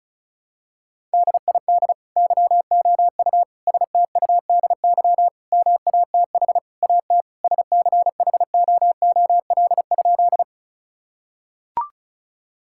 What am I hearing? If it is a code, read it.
Morse code, heard as a keyed tone, about 34 wpm, 700 Hz. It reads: DID YOU STUDY MATH AT SCHOOL?
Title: Sentences from Top 1000 Words - Review Part 6 - 35wpm